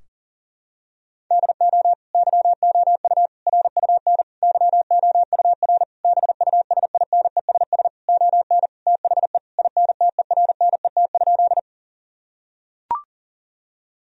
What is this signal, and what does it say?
Morse code, heard as a keyed tone, about 40 wpm, 700 Hz. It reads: DO YOU RUN YOUR BUSINESS ON THE INTERNET?
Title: Sentences from Top 1000 Words - Review Part 4 - 40wpm